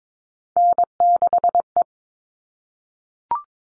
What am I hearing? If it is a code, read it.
Morse code, heard as a keyed tone, about 22 wpm, 700 Hz. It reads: N6E
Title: All Cut Numbers - 3-Digit 22wpm